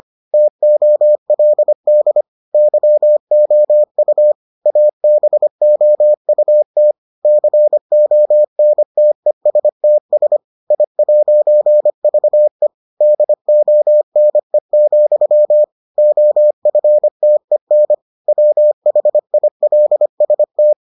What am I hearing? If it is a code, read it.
Morse code, heard as a keyed tone, about 25 wpm, 600 Hz. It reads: TOLD YOU ABOUT CONTESTS I'VE DONE, OFTEN WHILST